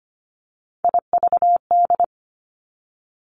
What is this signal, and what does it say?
Morse code, heard as a keyed tone, about 25 wpm, 700 Hz. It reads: IVD
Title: Three Challenging Letter-Number Combinations 25wpm